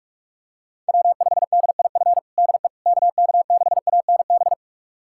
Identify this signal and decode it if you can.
Morse code, heard as a keyed tone, about 46 wpm, 700 Hz. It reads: W5DIF DE KK6ANB
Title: USA Callsign DE Callsign 45wpm